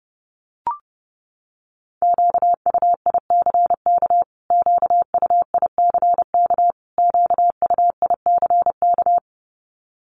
Morse code, heard as a keyed tone, about 30 wpm, 700 Hz. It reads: QUICK QUICK QUICK
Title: Top 500 Words Repeated 30wpm